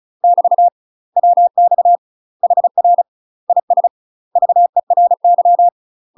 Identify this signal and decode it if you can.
Morse code, heard as a keyed tone, about 35 wpm, 700 Hz. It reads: <BT> WX HR IS VERY
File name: UK MASTER QSOs TRAINING / 35WPM / UK MASTER QSOs 0029